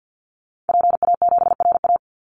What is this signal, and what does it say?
Morse code, heard as a keyed tone, about 50 wpm, 700 Hz. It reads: PA7RA
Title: Call Signs Repeated 3x - 50wpm